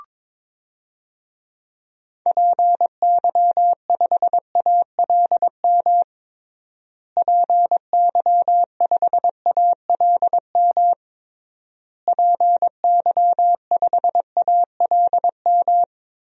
Morse code, heard as a keyed tone, about 22 wpm, 700 Hz. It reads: PY5ALM PY5ALM PY5ALM
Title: DXCC-100 Call Signs - 6 Characters - Repeated 3x - 22wpm